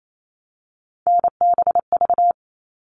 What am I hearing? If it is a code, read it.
Morse code, heard as a keyed tone, about 28 wpm, 700 Hz. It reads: NBV
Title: All Cut Numbers - 3-Digit 28wpm